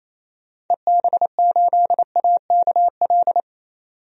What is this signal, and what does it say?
Morse code, heard as a keyed tone, about 28 wpm, 700 Hz. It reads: EB8AKL
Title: DXCC-100 Call Signs - 6 Characters - 28wpm